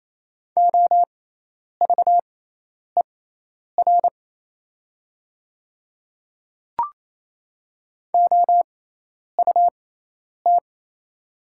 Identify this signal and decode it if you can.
Morse code, heard as a keyed tone, about 28 wpm, spaced as for 10 wpm, 700 Hz. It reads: OVER OUT
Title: 6x Character Spacing - Top 300 Words - 28wpm